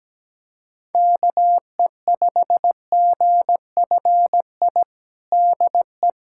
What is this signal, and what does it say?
Morse code, heard as a keyed tone, about 17 wpm, 700 Hz. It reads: KE5GFI DE
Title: USA Callsign DE Callsign 17wpm